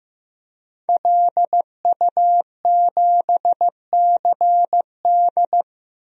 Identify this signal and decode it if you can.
Morse code, heard as a keyed tone, about 15 wpm, 700 Hz. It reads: LU7CD
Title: DXCC-100 Call Signs - 5 Characters - Repeated 3x - 15wpm